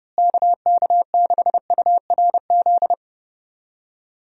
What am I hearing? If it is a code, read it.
Morse code, heard as a keyed tone, about 30 wpm, 700 Hz. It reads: KK6URZ
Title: USA Call Signs - 6 Characters - Repeated 3x - 30wpm